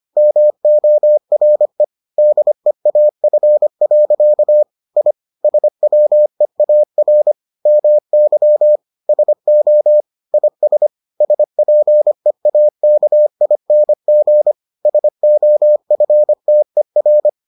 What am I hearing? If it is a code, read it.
Morse code, heard as a keyed tone, about 25 wpm, 600 Hz. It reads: MORE DEAF. I SWEAR MY SO IS SPEAKING SOFTER